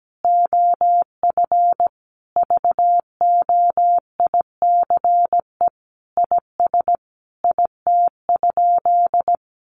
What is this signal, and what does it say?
Morse code, heard as a keyed tone, about 17 wpm, 700 Hz. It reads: OF VOICE IS IT?
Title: Sentences from Top 500 Words 17wpm